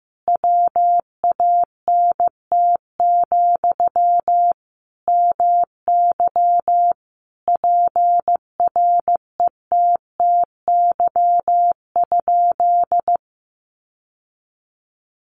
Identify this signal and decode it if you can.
Morse code, heard as a keyed tone, about 15 wpm, 700 Hz. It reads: WANT, MY PRETTY?
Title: Sentences from Top 1000 Words 15wpm